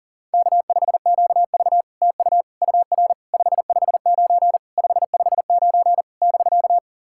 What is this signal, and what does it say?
Morse code, heard as a keyed tone, about 44 wpm, 700 Hz. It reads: K5QV TU UR 559 559 <BK>
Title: SOTA QSO Practice 40wpm